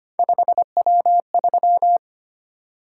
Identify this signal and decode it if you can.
Morse code, heard as a keyed tone, about 25 wpm, 700 Hz. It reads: 5W3
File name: Three Letter-Number Combinations - 25wpm